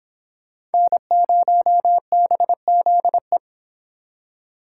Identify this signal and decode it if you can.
Morse code, heard as a keyed tone, about 29 wpm, 700 Hz. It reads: N0BZE